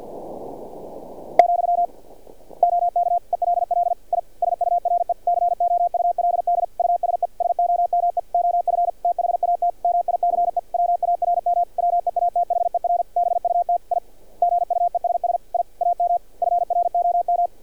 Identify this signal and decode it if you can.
Morse code, heard as a keyed tone, about 46 wpm, 700 Hz. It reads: , OK ELL I SURE COULD USE SOME OF THAT MICE WARM WEATHER BUT I GUESS I AM FROM
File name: archived / CRSnet